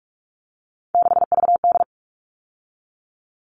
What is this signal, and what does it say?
Morse code, heard as a keyed tone, about 44 wpm, 700 Hz. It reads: 6VD
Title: All Cut Numbers - 3-Digit 45wpm